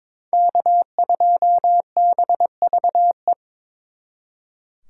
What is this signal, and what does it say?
Morse code, heard as a keyed tone, about 26 wpm, 700 Hz. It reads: K2BVE
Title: Speed Racing - USA 1x3 Call Signs 22wpm